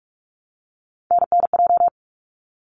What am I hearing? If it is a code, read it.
Morse code, heard as a keyed tone, about 44 wpm, 700 Hz. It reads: NNJ
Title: ARRL Sections 45wpm